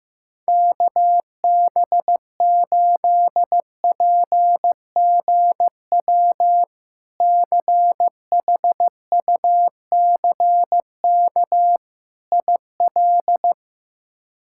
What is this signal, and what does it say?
Morse code, heard as a keyed tone, about 15 wpm, 700 Hz. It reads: KB8PGW CHUCK IL
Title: K1USN SST QSO Practice - 15wpm